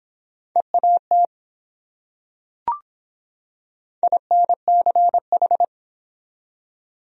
Morse code, heard as a keyed tone, about 24 wpm, 700 Hz. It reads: EAT INCH